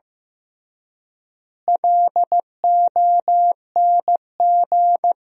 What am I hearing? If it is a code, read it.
Morse code, heard as a keyed tone, about 15 wpm, 700 Hz. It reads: LONG
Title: 3x Word-Spacing - Sentences from Top 300 Words - 15wpm